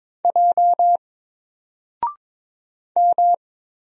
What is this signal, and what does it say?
Morse code, heard as a keyed tone, about 22 wpm, 700 Hz. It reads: J M
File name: Single-Letter-Number-22wpm